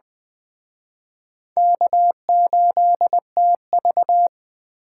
Morse code, heard as a keyed tone, about 19 wpm, 700 Hz. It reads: K8TV